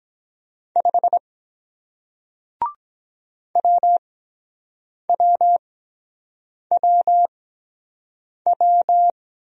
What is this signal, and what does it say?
Morse code, heard as a keyed tone, about 23 wpm, spaced as for 7 wpm, 700 Hz. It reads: 5 WWWW